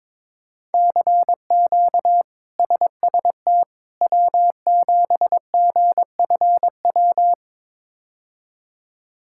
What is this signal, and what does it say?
Morse code, heard as a keyed tone, about 22 wpm, 700 Hz. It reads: CQ SST W7GFW
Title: K1USN SST QSO Practice - 22wpm